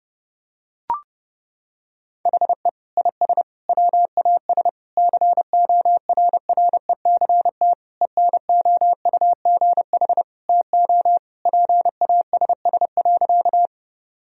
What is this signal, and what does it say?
Morse code, heard as a keyed tone, about 30 wpm, 700 Hz. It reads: HE IS WAS CORRECT ENOUGH TO PASS.
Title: Sentences from Top 1000 Words - Review Part 2 - 30wpm